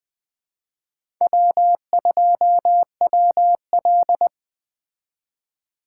Morse code, heard as a keyed tone, about 20 wpm, 700 Hz. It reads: W2WL
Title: Speed Racing - USA 1x2 Call Signs 17wpm